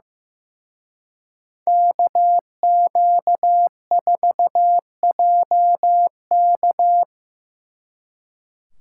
Morse code, heard as a keyed tone, about 18 wpm, 700 Hz. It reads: KQ4JK
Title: Speed Racing - USA 2x2 Call Signs 15wpm